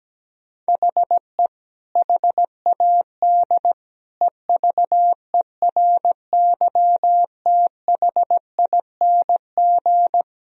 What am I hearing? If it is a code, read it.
Morse code, heard as a keyed tone, about 17 wpm, 700 Hz. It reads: HE HAD EVERYTHING
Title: Sentences from Top 1000 Words - Review Part 3 - 17wpm